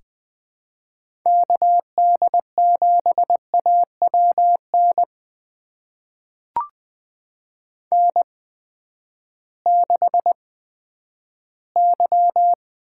Morse code, heard as a keyed tone, about 20 wpm, spaced as for 11 wpm, 700 Hz. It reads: KD7AWN N6Y
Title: USA Call Signs - 8x Character Spacing - 20wpm